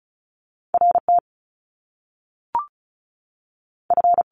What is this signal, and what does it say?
Morse code, heard as a keyed tone, about 36 wpm, spaced as for 6 wpm, 700 Hz. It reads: RTF
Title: Top Two Letter Combinations in Words 35wpm